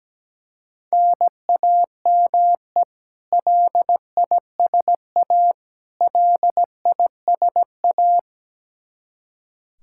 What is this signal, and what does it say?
Morse code, heard as a keyed tone, about 17 wpm, 700 Hz. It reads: NAME LISA LISA
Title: Practice QSO - Exchange 1 - Beginning - 17wpm